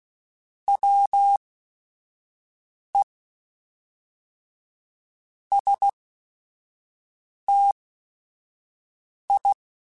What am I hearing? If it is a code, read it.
Morse code, heard as a keyed tone, about 16 wpm, spaced as for 4 wpm, 800 Hz. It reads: WEST I
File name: QSOs / QSOs-mp3s / WA7QOX 1@5